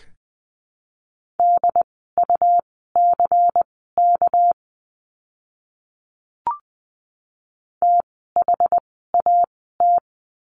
Morse code, heard as a keyed tone, about 20 wpm, spaced as for 14 wpm, 700 Hz. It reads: DUCK THAT
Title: 199 Lesson 32f - Letter k - Words 20wpm